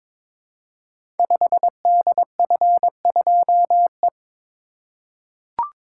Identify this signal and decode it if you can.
Morse code, heard as a keyed tone, about 22 wpm, 700 Hz. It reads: 5DF2E
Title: Code Groups - 5 Letter:Numbers - 22wpm